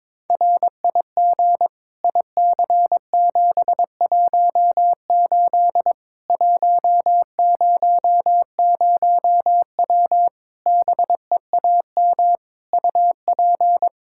Morse code, heard as a keyed tone, about 22 wpm, 700 Hz. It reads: RIG IC718 100W BEAM UP
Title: Practice QSO - Exchange 2 - Info - 22wpm